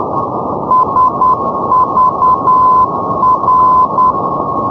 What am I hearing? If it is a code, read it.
Morse code, heard as a keyed tone, about 10 wpm, 1020 Hz, with radio noise. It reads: SVR
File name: europe / hng / hng-670 ndb sagvar svr-348+srb-78-tpl-348+hrv-240-sk-350-np-aut-poellau-2011-rb